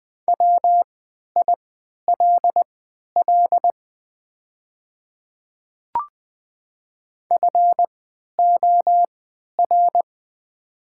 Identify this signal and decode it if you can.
Morse code, heard as a keyed tone, about 20 wpm, spaced as for 11 wpm, 690 Hz. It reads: WILL FOR
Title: 3x Character Spacing - Top 100 Words - 20wpm